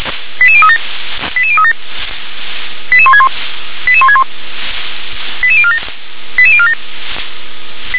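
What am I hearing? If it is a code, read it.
Five-tone selective call (ZVEI), heard as quick runs of five tones, tones of about 70 ms. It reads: 88037 88037 88161 88161 88047 88047
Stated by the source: Five-tone ZVEI call received on an SDR.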